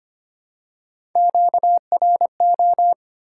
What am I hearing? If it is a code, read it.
Morse code, heard as a keyed tone, about 25 wpm, 700 Hz. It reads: QRO
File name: Top 90 QSO Elements - 3x Character Spacing - 25wpm